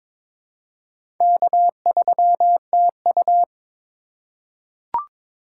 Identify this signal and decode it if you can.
Morse code, heard as a keyed tone, about 22 wpm, 700 Hz. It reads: K3TU
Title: USA Call Signs - 4 Characters - 4x Character Spacing - 22wpm